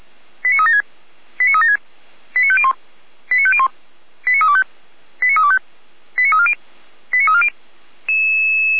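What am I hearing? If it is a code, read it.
Five-tone selective call (ZVEI), heard as quick runs of five tones, tones of about 80 ms. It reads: 89376 89376 89501 89501 89325 89325 89340 89340
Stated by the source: Five-tone ZVEI selective call received by SDR.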